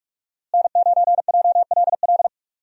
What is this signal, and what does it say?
Morse code, heard as a keyed tone, about 42 wpm, 700 Hz. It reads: N9JLL
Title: USA Call Signs - 5 Characters - Repeated 3x - 45wpm